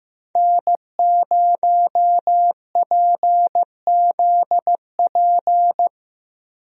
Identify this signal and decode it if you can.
Morse code, heard as a keyed tone, about 15 wpm, 700 Hz. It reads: N0PZP
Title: USA Call Signs - 5 Characters - Repeated 3x - 15wpm